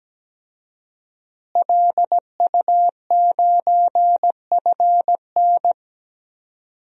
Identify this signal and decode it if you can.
Morse code, heard as a keyed tone, about 17 wpm, 700 Hz. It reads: LU9FN
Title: DXCC-100 Call Signs - 5 Characters - Repeated 3x - 17wpm